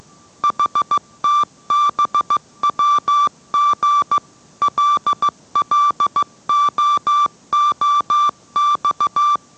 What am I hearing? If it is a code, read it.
Morse code, heard as a keyed tone, about 15 wpm, 1170 Hz. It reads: HTBWG LLOOX